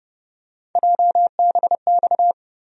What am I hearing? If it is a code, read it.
Morse code, heard as a keyed tone, about 30 wpm, 700 Hz. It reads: JBX
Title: USA Call Sign Postfixes - 3 Letters 30wpm